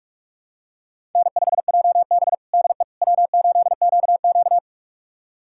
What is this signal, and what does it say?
Morse code, heard as a keyed tone, about 46 wpm, 700 Hz. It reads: N5JB DE W8QQ
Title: USA Callsign DE Callsign - 4 Characters - 45wpm